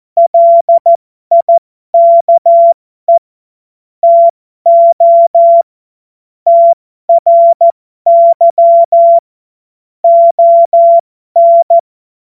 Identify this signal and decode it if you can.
Morse code, heard as a keyed tone, about 14 wpm, spaced as for 12 wpm, 680 Hz. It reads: LIKE TO TRY ON
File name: Morsekurs / Witze